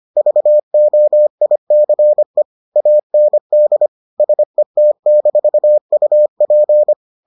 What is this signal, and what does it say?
Morse code, heard as a keyed tone, about 25 wpm, 600 Hz. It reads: VOICE AND SET-UP